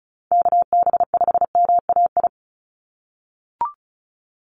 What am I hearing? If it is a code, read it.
Morse code, heard as a keyed tone, about 34 wpm, 700 Hz. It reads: KB5MAI
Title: USA Call Signs 35wpm